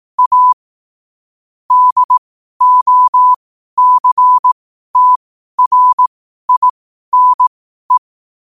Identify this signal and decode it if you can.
Morse code, heard as a keyed tone, about 18 wpm, spaced as for 12 wpm, 1000 Hz. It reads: A DOCTRINE
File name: qotdfiles / 12wpm